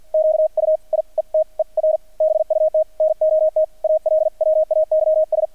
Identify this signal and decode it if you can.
Morse code, heard as a keyed tone, about 48 wpm, spaced as for 41 wpm, 630 Hz, with radio noise. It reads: QUIETER BUT NOT ALWAYS
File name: archived / CRSnet